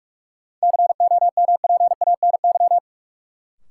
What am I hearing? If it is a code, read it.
Morse code, heard as a keyed tone, about 43 wpm, 700 Hz. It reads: COMPANY
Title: Sentences from Top 600 Words - New Word Drill - 45wpm